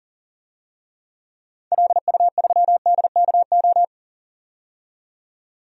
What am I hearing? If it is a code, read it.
Morse code, heard as a keyed tone, about 40 wpm, 700 Hz. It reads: LU3DKO
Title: DXCC-100 Call Signs - 6 Characters - 40wpm